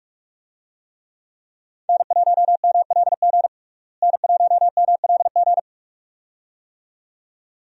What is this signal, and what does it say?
Morse code, heard as a keyed tone, about 43 wpm, 700 Hz. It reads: N1MLG N1MLG
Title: USA Call Signs - 5 Characters - Repeated 2x - 45wpm